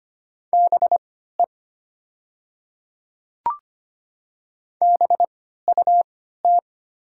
Morse code, heard as a keyed tone, about 25 wpm, spaced as for 14 wpm, 700 Hz. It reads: BE BUT
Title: 3x Character Spacing - Top 100 Words - 25wpm